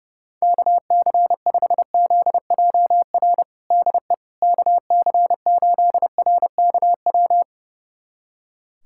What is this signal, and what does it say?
Morse code, heard as a keyed tone, about 30 wpm, 700 Hz. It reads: KC5ZJR DE KC8RKW
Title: USA Callsign DE Callsign - 6 Characters - 30wpm